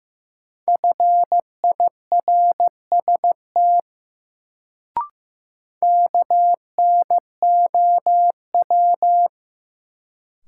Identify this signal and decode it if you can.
Morse code, heard as a keyed tone, about 15 wpm, 700 Hz. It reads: FIRST KNOW
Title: Sets of 2 Words - New Word Drill - 15wpm